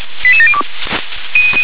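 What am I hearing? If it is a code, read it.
Five-tone selective call (ZVEI), heard as quick runs of five tones, tones of about 80 ms. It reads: 88702
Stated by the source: ZVEI five-tone selective call received off-air on an SDR.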